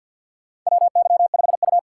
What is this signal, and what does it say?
Morse code, heard as a keyed tone, about 52 wpm, 700 Hz. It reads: WY5U
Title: USA Call Signs - 2x Character Spacing - 50wpm